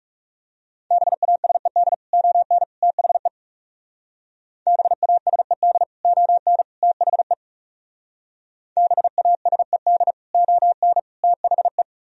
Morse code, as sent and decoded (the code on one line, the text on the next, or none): -... .- ... -.. / --- -. / - .... / -... .- ... -.. / --- -. / - .... / -... .- ... -.. / --- -. / - .... .
BASED ON THE BASED ON THE BASED ON THE